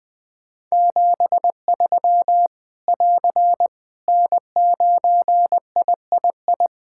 Morse code, heard as a keyed tone, about 20 wpm, 700 Hz. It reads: 73 <AR> N9III
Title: Practice QSO - Exchange 4 - Ending - 20wpm